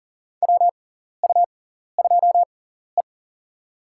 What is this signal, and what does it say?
Morse code, heard as a keyed tone, about 40 wpm, spaced as for 14 wpm, 700 Hz. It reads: WU2E
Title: USA Call Signs - 4 Characters - 6x Character Spacing - 40wpm